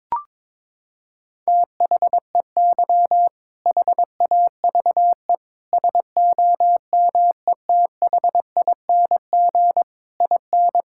Morse code, heard as a keyed tone, about 22 wpm, 700 Hz. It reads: THEY HAVE SOMETHING IN
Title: Sentences from Top 1000 Words - Review Part 5 - 22wpm